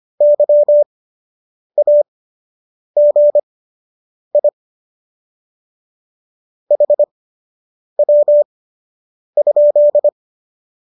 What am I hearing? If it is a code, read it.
Morse code, heard as a keyed tone, about 25 wpm, spaced as for 8 wpm, 590 Hz. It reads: YAGI HW?